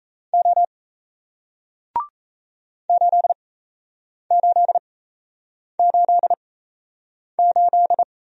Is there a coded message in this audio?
Yes: Morse code from 0.3 to 8.0 s, heard as a keyed tone, about 37 wpm, spaced as for 8 wpm, 700 Hz. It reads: O 8888